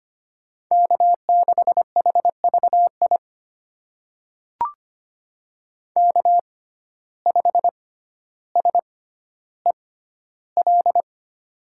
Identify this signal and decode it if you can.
Morse code, heard as a keyed tone, about 25 wpm, spaced as for 13 wpm, 700 Hz. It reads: K6HVI K5SEL
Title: USA Call Signs - 5 Characters - 6x Character Spacing - 25wpm